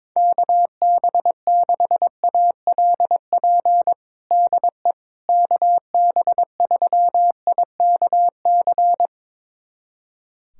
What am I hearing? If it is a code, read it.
Morse code, heard as a keyed tone, about 21 wpm, 700 Hz. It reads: KB6ALP DE KB3IKC
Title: USA Callsign DE Callsign 22wpm